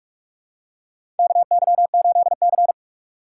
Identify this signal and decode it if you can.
Morse code, heard as a keyed tone, about 46 wpm, 700 Hz. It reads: KY8C
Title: USA Call Signs - 4x Character Spacing - 45wpm